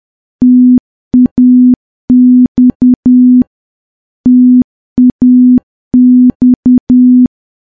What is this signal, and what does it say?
Morse code, heard as a keyed tone, about 10 wpm, 250 Hz. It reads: TAX TAX
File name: morse / morse-training-10WPM